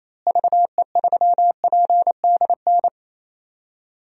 Morse code, heard as a keyed tone, about 28 wpm, 700 Hz. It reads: VE3PDN
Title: DXCC-100 Call Signs - 6 Characters - Repeated 3x - 28wpm